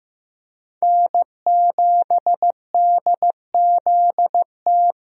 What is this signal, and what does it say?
Morse code, heard as a keyed tone, about 15 wpm, 700 Hz. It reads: N7DZT
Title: USA Call Signs - 5 Characters - Repeated 2x - 15wpm